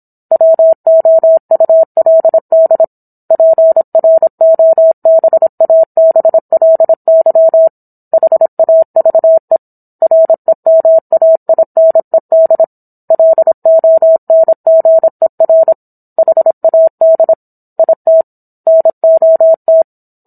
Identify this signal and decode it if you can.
Morse code, heard as a keyed tone, about 27 wpm, 650 Hz. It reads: WOULD PROBABLY HAVE REMAINED LONGER HAD IT NOT